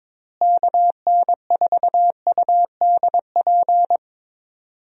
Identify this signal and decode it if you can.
Morse code, heard as a keyed tone, about 22 wpm, 700 Hz. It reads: KN4UDP
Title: USA Call Signs - 6 Characters - Repeated 3x - 22wpm